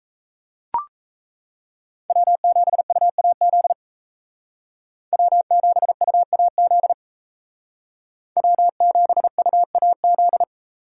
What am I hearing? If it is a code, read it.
Morse code, heard as a keyed tone, about 35 wpm, 700 Hz. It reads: W7UAZ W7UAZ W7UAZ
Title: Speed Racing - USA 1x3 Call Signs 28wpm